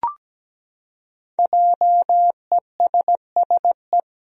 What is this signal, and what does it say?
Morse code, heard as a keyed tone, about 17 wpm, 700 Hz. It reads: JESSE
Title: USA Common Names - Max 5 Letters - 17wpm